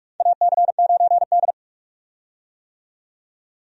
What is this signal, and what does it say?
Morse code, heard as a keyed tone, about 48 wpm, 700 Hz. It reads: AC9D
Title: USA Call Signs - 4 Characters - 45wpm